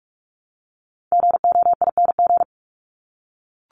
Morse code, heard as a keyed tone, about 44 wpm, 700 Hz. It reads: GOING
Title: ING Suffix - Top 100 Words with Base Words 45wpm